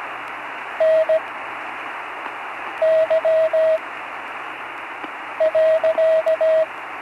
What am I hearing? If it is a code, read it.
Morse code, heard as a keyed tone, about 17 wpm, spaced as for 5 wpm, 640 Hz, with radio noise. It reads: NY.
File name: CW Learning Programs / Learning Morse Code (from Tune in the World) / 35 answers